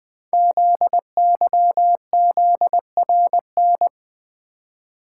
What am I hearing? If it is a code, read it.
Morse code, heard as a keyed tone, about 20 wpm, 700 Hz. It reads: ZYZRN
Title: Code Groups - 5 Letters - 20wpm